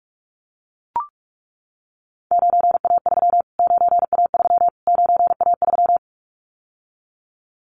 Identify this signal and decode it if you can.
Morse code, heard as a keyed tone, about 44 wpm, 700 Hz. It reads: 9A3 9A3 9A3
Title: Callsign Prefixes for Top 25 DXCC - Repeated 3x - 45wpm